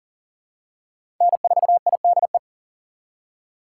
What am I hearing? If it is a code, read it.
Morse code, heard as a keyed tone, about 40 wpm, 700 Hz. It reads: N4IDE
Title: USA Call Signs - 3x Character Spacing - 40wpm